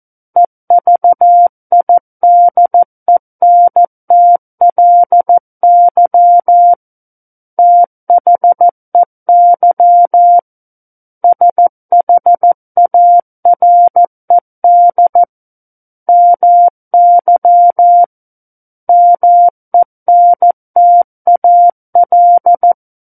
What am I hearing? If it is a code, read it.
Morse code, heard as a keyed tone, about 14 wpm, 700 Hz. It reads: EVIDENTLY THEY SHARED MY MENTAL
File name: Book One / 05-Chapter 5 - 14 WPM